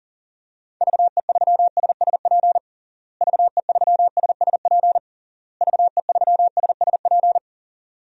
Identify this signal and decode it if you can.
Morse code, heard as a keyed tone, about 40 wpm, 700 Hz. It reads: VE3SSP VE3SSP VE3SSP
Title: DXCC-100 Call Signs - 6 Characters - Repeated 3x - 40wpm